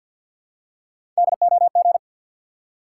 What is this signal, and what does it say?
Morse code, heard as a keyed tone, about 53 wpm, 700 Hz. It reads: DOG
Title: Three Letter Words - 50wpm